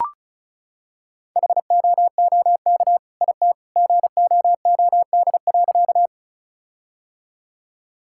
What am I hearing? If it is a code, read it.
Morse code, heard as a keyed tone, about 36 wpm, 700 Hz. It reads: HOOK IT GOOD.
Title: Sentences from Top 1600 Words 35wpm